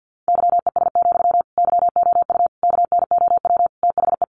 Morse code, heard as a keyed tone, about 50 wpm, 700 Hz. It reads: YES, YOU KNOW THE
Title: Sentences from Top 700 Words 50wpm